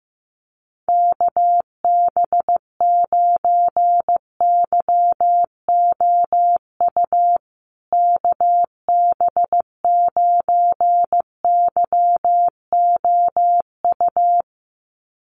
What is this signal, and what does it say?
Morse code, heard as a keyed tone, about 15 wpm, 700 Hz. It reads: KB9YOU KB9YOU
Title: USA Call Signs Repeated 2x 15wpm